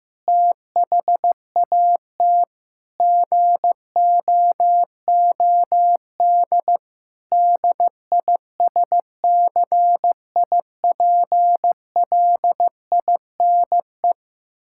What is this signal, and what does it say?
Morse code, heard as a keyed tone, about 15 wpm, 700 Hz. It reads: THAT GOOD DISCIPLINE
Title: Sentences from Top 1000 Words - Review Part 6 - 15wpm